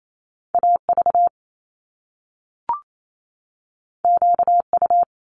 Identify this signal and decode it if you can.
Morse code, heard as a keyed tone, about 28 wpm, 700 Hz. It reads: AV QU
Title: Top Two Letter Combinations in Words 28wpm